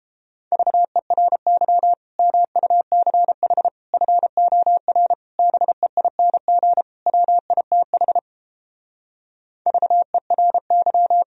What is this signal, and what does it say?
Morse code, heard as a keyed tone, about 31 wpm, 700 Hz. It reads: VERY MUCH FOR BEING WITH VERY